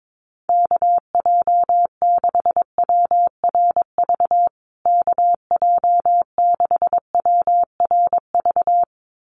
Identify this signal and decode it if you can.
Morse code, heard as a keyed tone, about 22 wpm, 700 Hz. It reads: KJ6WRV KJ6WRV
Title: USA Call Signs Repeated 2x 22wpm